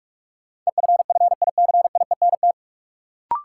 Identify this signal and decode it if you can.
Morse code, heard as a keyed tone, about 46 wpm, 700 Hz. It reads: EFFICIENT
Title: Sentences from Top 1300 Words - New Word Drill Alt - 45wpm